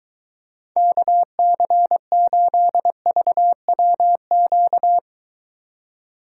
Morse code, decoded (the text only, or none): KC8VWQ